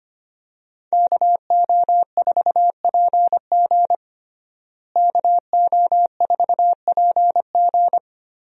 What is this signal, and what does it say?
Morse code, heard as a keyed tone, about 25 wpm, 700 Hz. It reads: KO4PG KO4PG
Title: USA Call Signs - 5 Characters - Repeated 3x - 25wpm